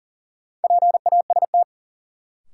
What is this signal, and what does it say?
Morse code, heard as a keyed tone, about 40 wpm, 700 Hz. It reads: PAST
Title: Sentences from Top 600 Words - New Word Drill - 40wpm